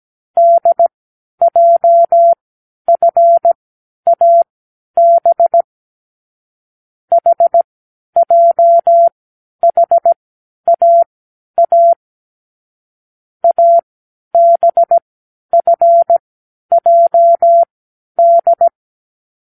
Morse code, heard as a keyed tone, about 17 wpm, spaced as for 10 wpm, 680 Hz. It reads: DJFAB HJHAA ABFJD
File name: k7qo / Lesson15(A-J)